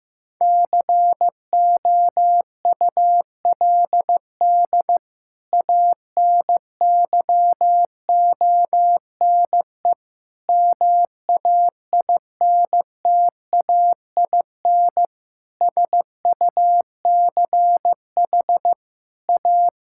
Morse code, heard as a keyed tone, about 15 wpm, 700 Hz. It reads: COULD ANYONE MAINTAIN SUCH A
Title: Sentences from Top 1000 Words - Review Part 5 - 15wpm